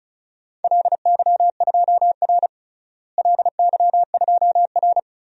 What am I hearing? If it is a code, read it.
Morse code, heard as a keyed tone, about 36 wpm, 700 Hz. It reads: LY2R LY2R
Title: DXCC-100 Call Signs - 4 Characters - Repeated 2x - 35wpm